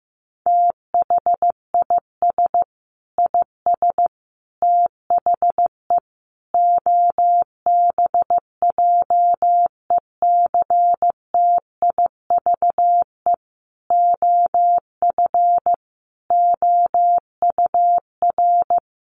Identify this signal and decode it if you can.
Morse code, heard as a keyed tone, about 14 wpm, 700 Hz. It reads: THIS IS THE OBJECTIVE OF OUR
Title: Sentences from Top 1200 Words 15wpm